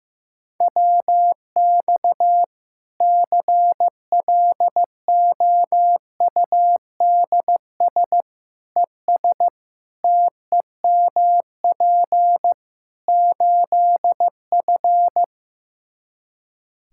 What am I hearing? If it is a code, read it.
Morse code, heard as a keyed tone, about 15 wpm, 700 Hz. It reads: WX CLOUDS ES TEMP 8F
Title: Practice QSO - Exchange 2 - Info - 15wpm